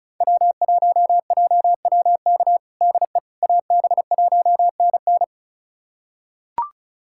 Morse code, heard as a keyed tone, about 36 wpm, 700 Hz. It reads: W1JWK DE AB1NN
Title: USA Callsign DE Callsign - 5 Characters - 35wpm